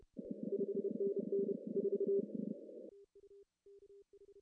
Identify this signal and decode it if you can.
Morse code, heard as a keyed tone, about 28 wpm, 400 Hz. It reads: 5NN 4
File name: CW / cqww-cw-k5tr